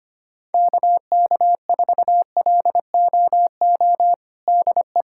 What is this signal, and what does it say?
Morse code, heard as a keyed tone, about 25 wpm, 700 Hz. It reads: KK4LOO DE
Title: USA Callsign DE Callsign 25wpm